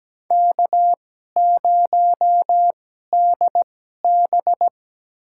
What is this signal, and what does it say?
Morse code, heard as a keyed tone, about 17 wpm, spaced as for 12 wpm, 700 Hz. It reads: K0DB
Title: USA Call Signs - 4 Characters - 2x Character Spacing - 17wpm